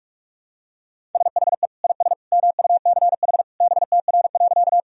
Morse code, heard as a keyed tone, about 43 wpm, 700 Hz. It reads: SHE IS MUCH BTR.
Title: Sentences from Top 400 Words and Abbreviations 45wpm